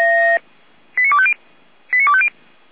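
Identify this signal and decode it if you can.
Five-tone selective call (ZVEI), heard as quick runs of five tones, tones of about 70 ms. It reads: 89250 89250
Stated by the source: ZVEI five-tone selective call received by SDR, fire brigade Feuerwehr Orlamünde.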